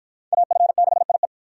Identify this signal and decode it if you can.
Morse code, heard as a keyed tone, about 46 wpm, 700 Hz. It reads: AF6IE